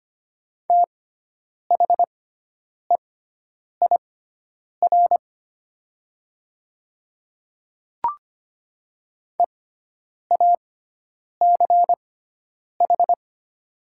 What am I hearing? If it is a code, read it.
Morse code, heard as a keyed tone, about 25 wpm, spaced as for 9 wpm, 700 Hz. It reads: THEIR EACH
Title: 6x Character Spacing - Top 100 Words - 25wpm